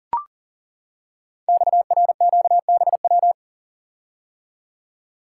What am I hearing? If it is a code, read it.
Morse code, heard as a keyed tone, about 42 wpm, 690 Hz. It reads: XRQBW
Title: Code Groups - 5 Letters - 40wpm